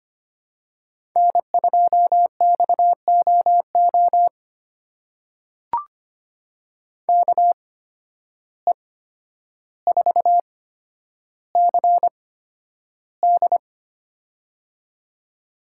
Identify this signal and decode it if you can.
Morse code, heard as a keyed tone, about 25 wpm, spaced as for 11 wpm, 700 Hz. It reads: N2XOO KE4CD
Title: USA Call Signs - 5 Characters - 8x Character Spacing - 25wpm